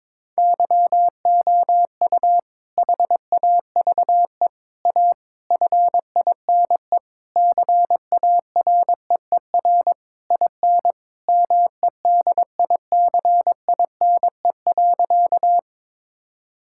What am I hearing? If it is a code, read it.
Morse code, heard as a keyed tone, about 22 wpm, 700 Hz. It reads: YOU HAVE A FINE CAREER IN MEDICINE.
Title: Sentences from Top 1000 Words - Review Part 6 - 22wpm